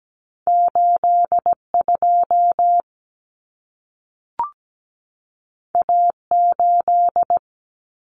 Morse code, heard as a keyed tone, about 17 wpm, 700 Hz. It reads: 82 A8
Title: Common Cut Numbers - 2-Digit 17wpm